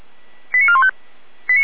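Five-tone selective call (ZVEI), heard as quick runs of five tones, tones of about 80 ms. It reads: 89415
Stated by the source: ZVEI five-tone selective call received on an SDR.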